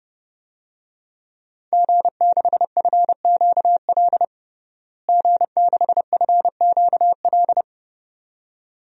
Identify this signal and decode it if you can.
Morse code, heard as a keyed tone, about 31 wpm, 700 Hz. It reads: G6FQL G6FQL
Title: DXCC-100 Call Signs - 5 Characters - Repeated 2x - 30wpm